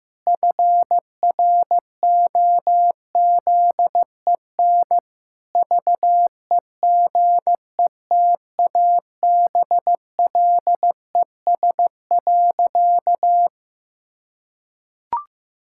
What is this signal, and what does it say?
Morse code, heard as a keyed tone, about 15 wpm, 700 Hz. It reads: FROZEN VEGETABLES.
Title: Pangram Challenge v2 - 15wpm